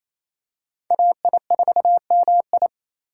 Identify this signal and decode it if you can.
Morse code, heard as a keyed tone, about 27 wpm, 700 Hz. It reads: AI4MI
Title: K1USN SST QSO Practice - 28wpm